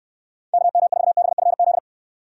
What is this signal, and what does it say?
Morse code, heard as a keyed tone, about 69 wpm, 700 Hz. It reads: KM4BFZ